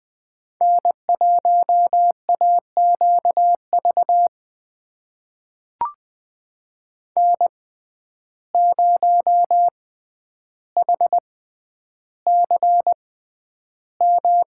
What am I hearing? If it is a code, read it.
Morse code, heard as a keyed tone, about 20 wpm, spaced as for 10 wpm, 700 Hz. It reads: N1AQV N0HCM
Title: USA Call Signs - 5 Characters - 6x Character Spacing - 20wpm